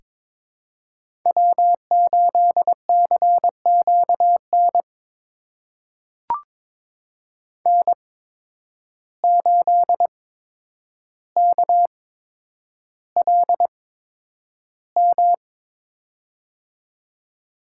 Morse code, heard as a keyed tone, about 23 wpm, spaced as for 10 wpm, 700 Hz. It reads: W8CQN N8KLM
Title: USA Call Signs - 5 Characters - 8x Character Spacing - 22wpm